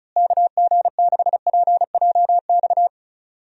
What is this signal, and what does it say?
Morse code, heard as a keyed tone, about 34 wpm, 700 Hz. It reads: KG6PJX
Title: USA Call Signs - 6 Characters - Repeated 2x - 35wpm